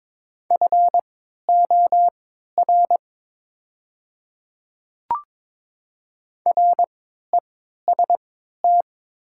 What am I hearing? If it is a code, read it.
Morse code, heard as a keyed tone, about 22 wpm, spaced as for 13 wpm, 700 Hz. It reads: FOR REST
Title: 3x Character Spacing - Top 500 Words - 22wpm